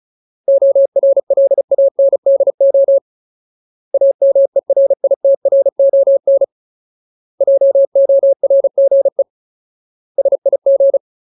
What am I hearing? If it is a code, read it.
Morse code, heard as a keyed tone, about 34 wpm, 550 Hz. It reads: ORLANDO AMERITRON JORGE SIG
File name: Ultimate Word List / 35WPM Ultimate